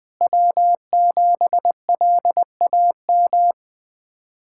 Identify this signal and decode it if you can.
Morse code, heard as a keyed tone, about 20 wpm, 700 Hz. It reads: W7LAM
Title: USA Callsign DE Callsign - 5 Characters - 20wpm